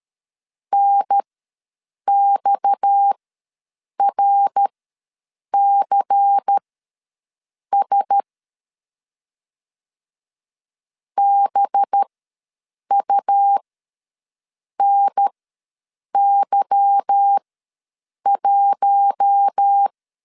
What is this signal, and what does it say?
Morse code, heard as a keyed tone, about 13 wpm, spaced as for 7 wpm, 800 Hz. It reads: NXRCS BUNY1
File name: LEZIONI / 8 YJ45 / 8 Y ws 12wpm 3409